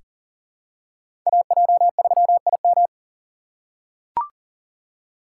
Morse code, heard as a keyed tone, about 40 wpm, 700 Hz. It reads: AJ3IM
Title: USA Call Signs - 5 Characters - 3x Character Spacing - 40wpm